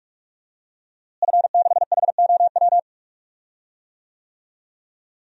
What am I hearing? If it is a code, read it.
Morse code, heard as a keyed tone, about 48 wpm, 700 Hz. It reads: F6HOW
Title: DXCC-100 Call Signs - 5 Characters - Repeated 2x - 45wpm